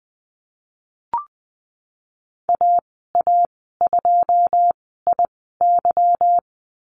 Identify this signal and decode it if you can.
Morse code, heard as a keyed tone, about 20 wpm, spaced as for 15 wpm, 700 Hz. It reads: AA2IY
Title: 154 Lesson 25g - Letter p - Call Signs 20wpm